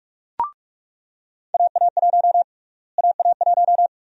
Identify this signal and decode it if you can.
Morse code, heard as a keyed tone, about 46 wpm, 700 Hz. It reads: AA1 AA1
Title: USA Call Sign Prefixes - Repeated 2x - 45wpm